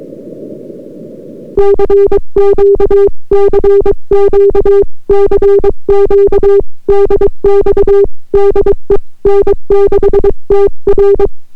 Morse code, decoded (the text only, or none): CQ CQ CQ DX DE N6TR